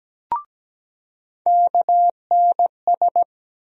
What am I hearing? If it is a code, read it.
Morse code, heard as a keyed tone, about 17 wpm, 700 Hz. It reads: KNS
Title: USA Call Sign Postfixes - 3 Letters 17wpm